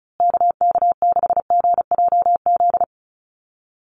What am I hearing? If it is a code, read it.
Morse code, heard as a keyed tone, about 34 wpm, 700 Hz. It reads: KK6GJZ
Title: USA Call Signs Repeated 3x 35wpm